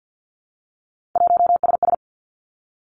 Morse code, heard as a keyed tone, about 50 wpm, 700 Hz. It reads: 1SS
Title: Three Challenging Letter-Number Combinations 50wpm